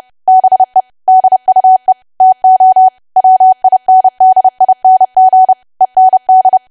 Morse code, heard as a keyed tone, about 30 wpm, 750 Hz. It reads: BE DUE TO WINDING END
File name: cwa / W1AW-Level3 / 30WPM